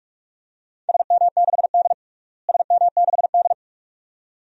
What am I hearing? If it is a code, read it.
Morse code, heard as a keyed tone, about 42 wpm, 700 Hz. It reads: SM6D SM6D
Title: DXCC-100 Call Signs - 4 Characters - Repeated 2x - 45wpm